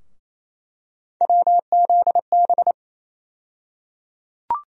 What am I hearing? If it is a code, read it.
Morse code, heard as a keyed tone, about 28 wpm, 700 Hz. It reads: WZB
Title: USA Call Sign Postfixes - 3 Letters 28wpm